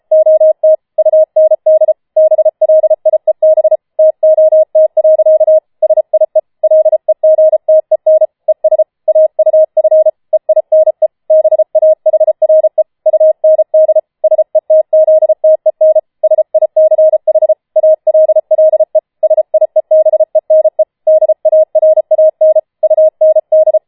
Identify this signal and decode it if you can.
Morse code, heard as a keyed tone, about 33 wpm, 630 Hz. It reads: OT UND BLIEB TOT. SIE LEGTEN ES AUF EINE BAHRE UND SETZTEN SICH ALLE SIEBENE DARAN UND